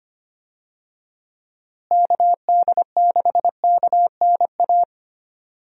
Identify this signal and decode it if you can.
Morse code, heard as a keyed tone, about 25 wpm, 700 Hz. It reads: KD6KNA
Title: USA Call Signs - 6 Characters - Repeated 2x - 25wpm